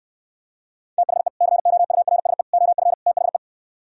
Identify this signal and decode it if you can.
Morse code, heard as a keyed tone, about 68 wpm, 700 Hz. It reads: THE COURSE OF THE